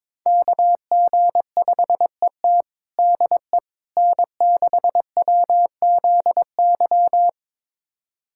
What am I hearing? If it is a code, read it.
Morse code, heard as a keyed tone, about 22 wpm, 700 Hz. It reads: KG5ET DE N6WZY
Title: USA Callsign DE Callsign - 5 Characters - 22wpm